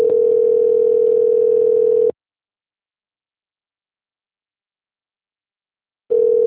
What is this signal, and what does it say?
Ringback tone (North America), 440 + 480 Hz.